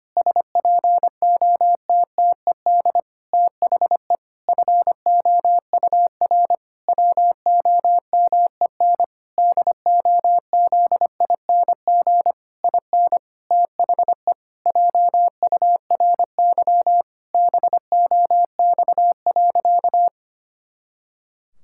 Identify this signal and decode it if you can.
Morse code, heard as a keyed tone, about 25 wpm, 700 Hz. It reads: SPOTTED THE FOUR WOMEN DOZING IN THE JURY BOX.
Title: Pangram Challenge v2 - 25wpm